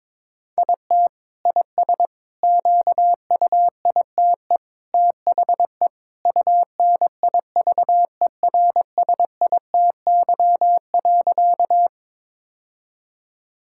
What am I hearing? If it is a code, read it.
Morse code, heard as a keyed tone, about 22 wpm, 700 Hz. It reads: IT IS QUITE THE UNIVERSITY.
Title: Sentences from Top 1000 Words - Review Part 5 - 22wpm